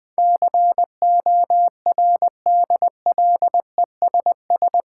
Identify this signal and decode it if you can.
Morse code, heard as a keyed tone, about 20 wpm, 700 Hz. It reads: CORDLESS
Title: Difficult Words with Hints - 20wpm